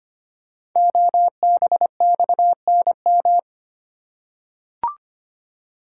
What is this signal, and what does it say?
Morse code, heard as a keyed tone, about 25 wpm, 700 Hz. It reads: OBXNM
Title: Code Groups - 5 Letters - 25wpm